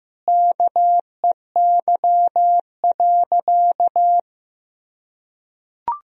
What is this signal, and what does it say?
Morse code, heard as a keyed tone, about 15 wpm, 700 Hz. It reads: KEY.
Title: Sentences from Top 1000 Words - Review Part 4 - 15wpm